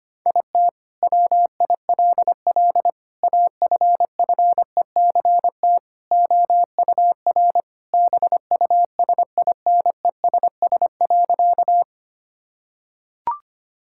Morse code, heard as a keyed tone, about 25 wpm, 700 Hz. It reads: IT WILL AFFECT OUR BUSINESS.
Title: Sentences from Top 1000 Words - Review Part 6 - 25wpm